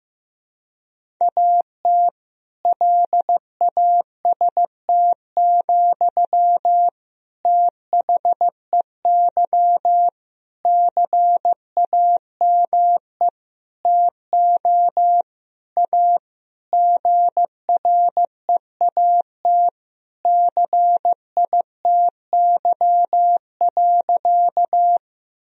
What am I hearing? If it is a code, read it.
Morse code, heard as a keyed tone, about 15 wpm, 700 Hz. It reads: AT LAST, THEY CAME TO A GREAT CITY.
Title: Sentences from Top 1000 Words - Review Part 1 - 15wpm